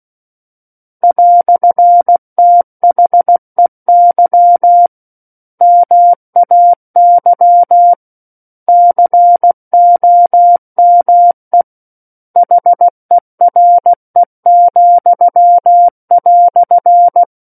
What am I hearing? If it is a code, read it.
Morse code, heard as a keyed tone, about 16 wpm, 700 Hz. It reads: "THEY MAY COME HERE,"
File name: Book One / 07-Chapter 7 - 16 WPM